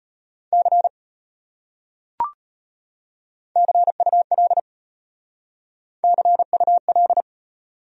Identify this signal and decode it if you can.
Morse code, heard as a keyed tone, about 33 wpm, 700 Hz. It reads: C CUL CUL